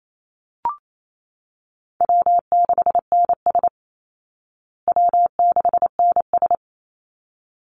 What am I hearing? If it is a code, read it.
Morse code, heard as a keyed tone, about 28 wpm, 700 Hz. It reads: W6NS W6NS
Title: Easy Call Signs Repeated 2x - 28wpm